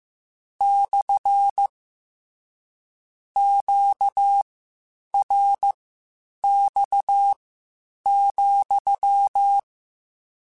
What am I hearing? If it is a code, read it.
Morse code, heard as a keyed tone, about 15 wpm, spaced as for 8 wpm, 800 Hz. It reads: / QRX,